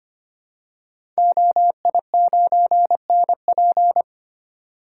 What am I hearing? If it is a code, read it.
Morse code, heard as a keyed tone, about 25 wpm, 700 Hz. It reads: OI9NP
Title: Code Groups - 5 Letter:Numbers - 25wpm